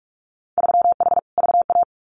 Morse code, heard as a keyed tone, about 46 wpm, 700 Hz. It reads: 3H VA
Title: Winter Field Day Exchanges 45wpm